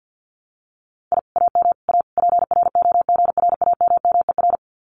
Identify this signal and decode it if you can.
Morse code, heard as a keyed tone, about 52 wpm, 700 Hz. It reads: I AM A PROGRAMMER
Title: Common QSO Phrases Encore 50wpm